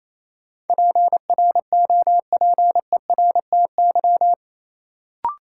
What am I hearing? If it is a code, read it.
Morse code, heard as a keyed tone, about 28 wpm, 700 Hz. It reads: PROPERTY
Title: Sentences from Top 1000 Words - New Word Drill - 28wpm